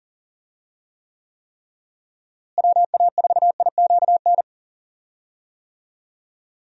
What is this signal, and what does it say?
Morse code, heard as a keyed tone, about 40 wpm, 700 Hz. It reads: WA4IQN
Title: USA Call Signs - 6 Characters - Repeated 3x - 40wpm